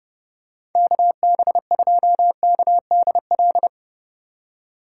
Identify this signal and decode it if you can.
Morse code, heard as a keyed tone, about 32 wpm, 700 Hz. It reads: KB2KDL
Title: USA Call Signs - 6 Characters - 3x Character Spacing - 30wpm